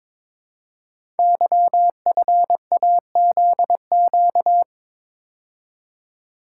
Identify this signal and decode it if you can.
Morse code, heard as a keyed tone, about 22 wpm, 700 Hz. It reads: YFAZQ
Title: Code Groups - 5 Letter:Numbers - 22wpm